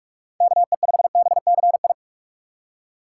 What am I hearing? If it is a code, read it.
Morse code, heard as a keyed tone, about 43 wpm, 700 Hz. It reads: KE5BCI
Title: USA Call Signs - 4x Character Spacing - 45wpm